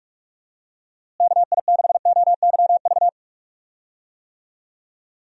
Morse code, heard as a keyed tone, about 43 wpm, 700 Hz. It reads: KI6OYV
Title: USA Call Signs - 6 Characters - Repeated 2x - 45wpm